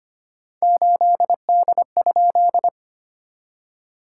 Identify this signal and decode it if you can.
Morse code, heard as a keyed tone, about 25 wpm, 700 Hz. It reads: 8D?
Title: SOTA QSO Practice 25wpm